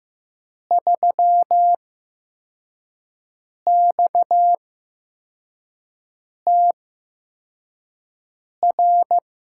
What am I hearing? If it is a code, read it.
Morse code, heard as a keyed tone, about 15 wpm, spaced as for 4 wpm, 700 Hz. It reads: 3XTR
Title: USA Call Signs - 5 Characters - 8x Character Spacing - 15wpm